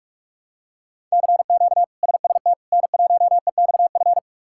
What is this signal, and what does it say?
Morse code, heard as a keyed tone, about 47 wpm, 700 Hz. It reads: CQ SST N1EXF
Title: K1USN SST QSO Practice - 45wpm